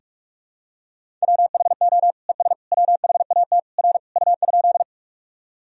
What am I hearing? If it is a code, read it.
Morse code, heard as a keyed tone, about 46 wpm, 700 Hz. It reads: WHO ES WHAT R U?
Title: Sentences from Top 100 Words and Abbreviations 45wpm